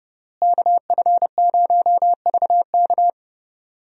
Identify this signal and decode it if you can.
Morse code, heard as a keyed tone, about 32 wpm, 700 Hz. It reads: KF0VK
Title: USA Call Signs - 5 Characters - Repeated 3x - 30wpm